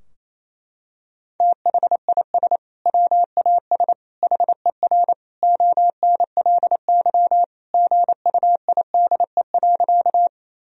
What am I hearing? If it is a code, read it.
Morse code, heard as a keyed tone, about 28 wpm, 700 Hz. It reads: THIS WAS HER ONLY GUIDE.
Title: Sentences from Top 1000 Words - Review Part 3 - 28wpm